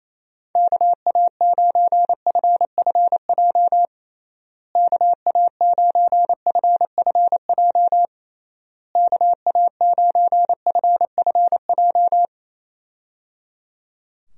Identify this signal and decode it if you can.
Morse code, heard as a keyed tone, about 28 wpm, 700 Hz. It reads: KA9FFJ KA9FFJ KA9FFJ
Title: USA Call Signs - 6 Characters - Repeated 3x - 28wpm